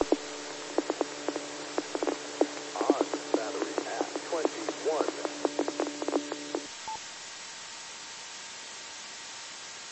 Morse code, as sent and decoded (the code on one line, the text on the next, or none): none